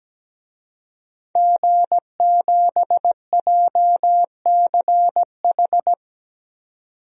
Morse code, heard as a keyed tone, about 17 wpm, 700 Hz. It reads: G7JCH
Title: DXCC-100 Call Signs - 5 Characters - 17wpm